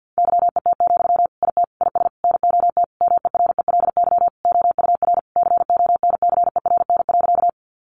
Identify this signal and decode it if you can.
Morse code, heard as a keyed tone, about 50 wpm, 700 Hz. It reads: YET, IT IS NOT MERELY OUR CONCERN.
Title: Sentences from Top 1400 Words 50wpm